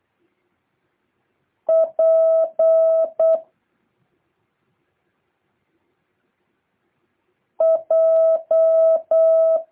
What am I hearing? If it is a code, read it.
Morse code, heard as a keyed tone, about 8 wpm, 650 Hz. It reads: P J
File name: jh8cbh / 2023cw